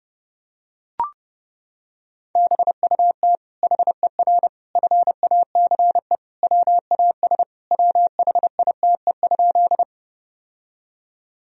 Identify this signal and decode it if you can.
Morse code, heard as a keyed tone, about 30 wpm, 700 Hz. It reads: BUT HER FACE WAS WHITE?
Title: Sentences from Top 1000 Words - Review Part 2 - 30wpm